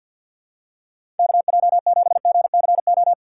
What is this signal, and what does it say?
Morse code, heard as a keyed tone, about 49 wpm, 700 Hz. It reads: KJ7GCO
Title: USA Call Signs - 6x Character Spacing - 50wpm